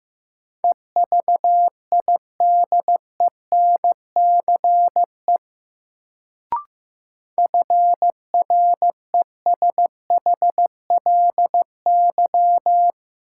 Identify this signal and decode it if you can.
Morse code, heard as a keyed tone, about 16 wpm, 700 Hz. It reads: EVIDENCE FRESHLY
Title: Sets of 5 Words Encore - New Word Drill - 15wpm